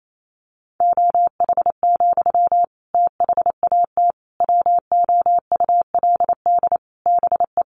Morse code, heard as a keyed tone, about 28 wpm, 700 Hz. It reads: OH, THAT WOULD BE
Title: Sentences from Top 1100 Words 28wpm